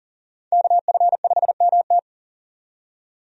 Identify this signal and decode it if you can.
Morse code, heard as a keyed tone, about 37 wpm, 700 Hz. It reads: KF5MT
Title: USA Call Signs - 5 Characters - 40wpm